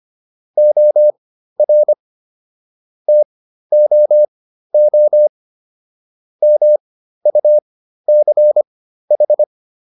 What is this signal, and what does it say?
Morse code, heard as a keyed tone, about 25 wpm, spaced as for 13 wpm, 600 Hz. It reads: OR TOO MUCH